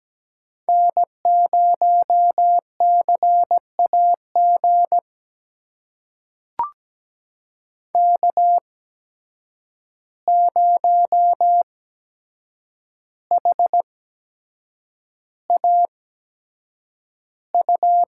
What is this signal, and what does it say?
Morse code, heard as a keyed tone, about 17 wpm, spaced as for 7 wpm, 700 Hz. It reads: N0CAG K0HAU
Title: USA Call Signs - 5 Characters - 8x Character Spacing - 17wpm